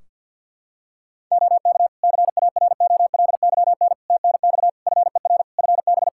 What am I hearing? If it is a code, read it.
Morse code, heard as a keyed tone, about 52 wpm, 700 Hz. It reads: OK CAROLYN TNX FER FB
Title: Practice QSO - Exchange 4 - Ending - 50wpm